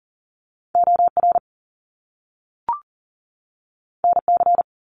Morse code, heard as a keyed tone, about 42 wpm, 700 Hz. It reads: OR NC
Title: Top Two Letter Combinations in Words 40wpm